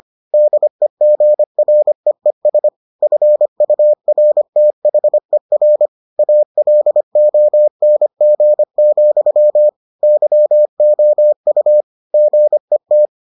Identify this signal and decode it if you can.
Morse code, heard as a keyed tone, about 25 wpm, 600 Hz. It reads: DEGREES FURTHER ALONG, YOU GET